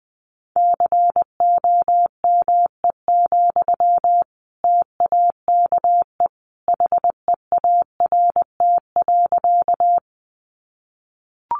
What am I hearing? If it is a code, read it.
Morse code, heard as a keyed tone, about 20 wpm, 700 Hz. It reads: COME, TAKE HEART.
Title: Sentences from Top 1200 Words 20wpm